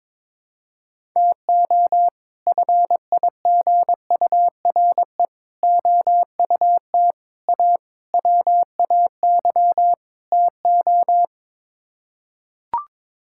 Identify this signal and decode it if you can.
Morse code, heard as a keyed tone, about 22 wpm, 700 Hz. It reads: TO FIGURE OUT A WAY TO
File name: practice / 22wpm / Sets of 6 Words - 22wpm